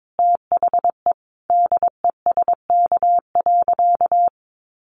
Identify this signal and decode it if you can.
Morse code, heard as a keyed tone, about 22 wpm, 700 Hz. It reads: THE DESK.
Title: A Sound Of Thunder 22wpm